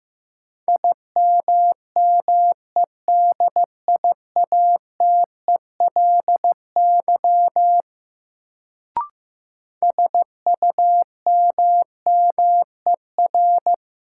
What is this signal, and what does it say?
Morse code, heard as a keyed tone, about 15 wpm, 700 Hz. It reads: IMMEDIATELY SUMMER
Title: Sentences from Top 1100 Words - New Word Drill Alt - 15wpm